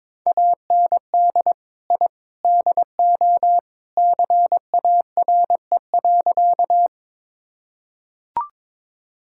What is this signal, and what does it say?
Morse code, heard as a keyed tone, about 22 wpm, 700 Hz. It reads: AND I DO CARE.
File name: practice / 22wpm / Sentences with 3-Word Phrases - 22wpm